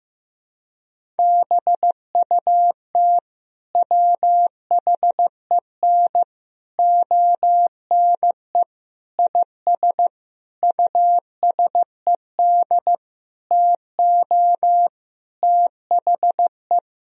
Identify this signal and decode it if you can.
Morse code, heard as a keyed tone, about 15 wpm, 700 Hz. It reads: BUT WHEN ONE IS USED TO THE
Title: Sentences from Top 1000 Words - Review Part 1 - 15wpm